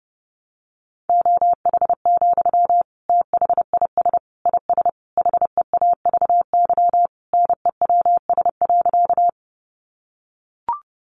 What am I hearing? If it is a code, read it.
Morse code, heard as a keyed tone, about 28 wpm, 700 Hz. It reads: OH, THIS IS HEAVY NEWS.
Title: Sentences from Top 1100 Words 30wpm